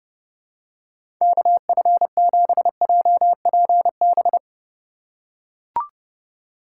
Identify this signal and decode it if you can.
Morse code, heard as a keyed tone, about 30 wpm, 700 Hz. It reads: KF7JPB
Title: USA Call Signs - 6 Characters - 2x Character Spacing - 30wpm